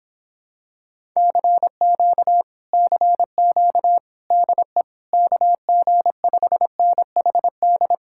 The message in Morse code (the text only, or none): CQ CQ DE KG5NHD